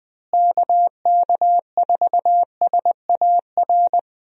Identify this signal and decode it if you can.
Morse code, heard as a keyed tone, about 20 wpm, 700 Hz. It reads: KK4SAR
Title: USA Call Signs - 6 Characters - Repeated 3x - 20wpm